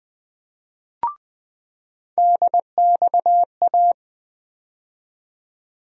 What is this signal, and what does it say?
Morse code, heard as a keyed tone, about 20 wpm, 700 Hz. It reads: DXA
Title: USA Call Sign Postfixes - 3 Letters 20wpm